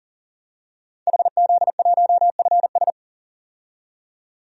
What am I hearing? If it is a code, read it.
Morse code, heard as a keyed tone, about 37 wpm, 700 Hz. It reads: HZ1FS
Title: DXCC-100 Call Signs - 5 Characters - 40wpm